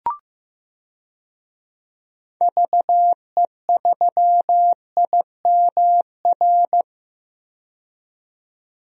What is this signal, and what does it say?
Morse code, heard as a keyed tone, about 15 wpm, 700 Hz. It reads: VE3IMR
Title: DXCC-100 Call Signs - 6 Characters - 15wpm